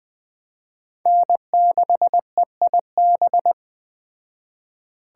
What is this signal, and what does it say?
Morse code, heard as a keyed tone, about 20 wpm, 700 Hz. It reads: N6EIB